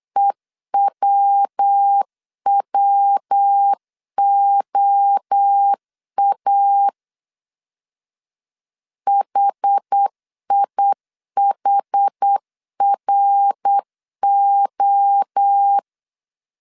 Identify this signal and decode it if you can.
Morse code, heard as a keyed tone, about 8 wpm, 800 Hz. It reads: EWWOA HIHRO
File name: CW / DET / 1 3 1500C S B 8wpm 4141